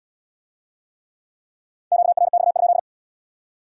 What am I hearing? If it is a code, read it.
Morse code, heard as a keyed tone, about 75 wpm, 700 Hz. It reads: QRX?